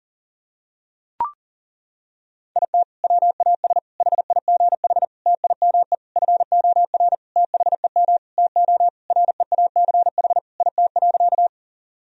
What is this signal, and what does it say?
Morse code, heard as a keyed tone, about 40 wpm, 700 Hz. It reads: IT WAS HIGH TIME FOR THEM TO REACH IT.
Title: Sentences from Top 1000 Words - Review Part 2 - 40wpm